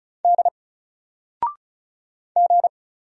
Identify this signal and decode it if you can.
Morse code, heard as a keyed tone, about 36 wpm, 700 Hz. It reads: D G
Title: Single Character (Letter:Number:Punctuation:Prosign) 35wpm